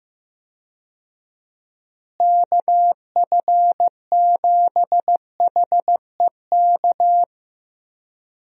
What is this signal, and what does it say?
Morse code, heard as a keyed tone, about 15 wpm, 700 Hz. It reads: KF7HEK
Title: USA Call Signs - 6 Characters - Repeated 3x - 15wpm